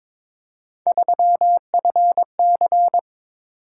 Morse code, heard as a keyed tone, about 21 wpm, 700 Hz. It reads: 3FC
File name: practice / 22wpm / Three Letter-Number Combinations - 22wpm